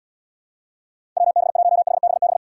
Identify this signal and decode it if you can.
Morse code, heard as a keyed tone, about 72 wpm, 700 Hz. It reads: WB1HBB